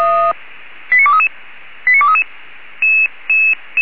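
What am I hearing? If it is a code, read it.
Five-tone selective call (ZVEI), heard as quick runs of five tones, tones of about 70 ms. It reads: 89233 89233
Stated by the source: ZVEI five-tone alert received on an SDR.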